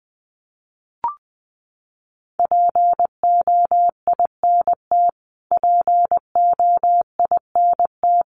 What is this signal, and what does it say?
Morse code, heard as a keyed tone, about 20 wpm, 700 Hz. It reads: POINT POINT
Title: Top 500 Words Repeated 20wpm